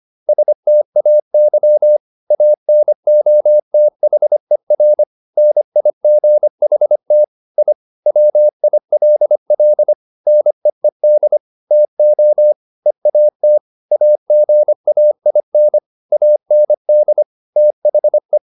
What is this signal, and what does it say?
Morse code, heard as a keyed tone, about 25 wpm, 600 Hz. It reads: STAY ANOTHER NIGHT I WILL NEED TO EAT AGAIN AND THE
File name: the old man and the sea part 1 25wpm